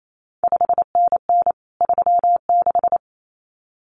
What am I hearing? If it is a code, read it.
Morse code, heard as a keyed tone, about 28 wpm, 700 Hz. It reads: 5NN 36
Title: CQ WW DX Contest Exchange Encore - 28wpm